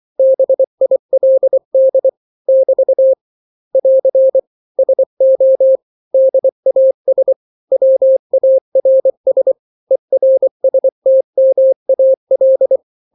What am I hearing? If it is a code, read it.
Morse code, heard as a keyed tone, about 24 wpm, 530 Hz. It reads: BILD <BT> <AR> SO DAS WARS ERSTMAL
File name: daily / 23wpm